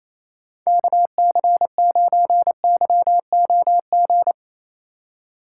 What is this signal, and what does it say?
Morse code, heard as a keyed tone, about 28 wpm, 700 Hz. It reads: KC9YOG
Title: USA Call Signs - 6 Characters - 2x Character Spacing - 28wpm